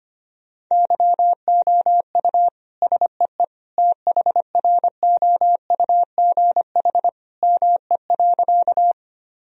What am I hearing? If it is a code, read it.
Morse code, heard as a keyed tone, about 25 wpm, 700 Hz. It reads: YOU SEE THROUGH ME.
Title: Sentences from Top 1000 Words - Review Part 1 - 25wpm